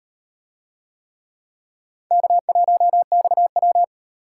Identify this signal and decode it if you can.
Morse code, heard as a keyed tone, about 38 wpm, 700 Hz. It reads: K1XW